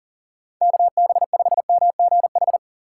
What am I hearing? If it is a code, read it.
Morse code, heard as a keyed tone, about 40 wpm, 690 Hz. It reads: KB5MGH
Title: USA Call Signs - 4x Character Spacing - 40wpm